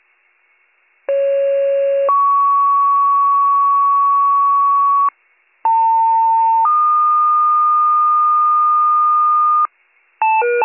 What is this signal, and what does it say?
Two-tone page, heard as a one-second tone, then a three-second tone, about 550 then 1120 Hz, about 900 then 1220 Hz. It is heard from 1.1 to 9.7 s.